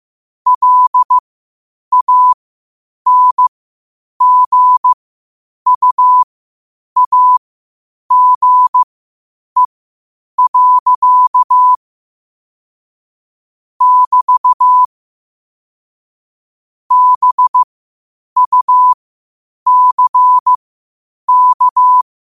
Morse code, heard as a keyed tone, about 15 wpm, spaced as for 8 wpm, 1000 Hz. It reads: LANGUAGE. <BT> BUCK
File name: qotdfiles / 8wpm